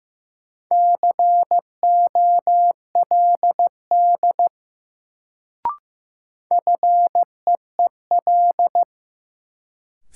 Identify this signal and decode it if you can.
Morse code, heard as a keyed tone, about 15 wpm, 700 Hz. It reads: COLD FEEL
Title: Sets of 5 Words - New Word Drill - 15wpm